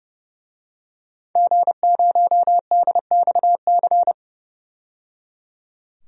Morse code, heard as a keyed tone, about 30 wpm, 700 Hz. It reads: G0DXC
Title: DXCC-100 Call Signs - 5 Characters - 30wpm